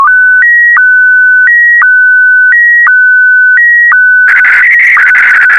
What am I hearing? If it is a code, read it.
SSTV (slow-scan television), Scottie DX.